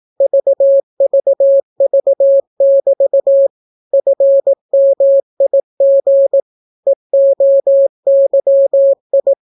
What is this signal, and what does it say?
Morse code, heard as a keyed tone, about 18 wpm, 550 Hz. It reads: VVV<BT> FMIG EOYI